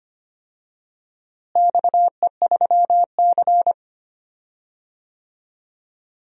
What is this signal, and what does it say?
Morse code, heard as a keyed tone, about 25 wpm, 700 Hz. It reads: XE3C
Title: DXCC-100 Call Signs - 4 Characters - Repeated 2x - 25wpm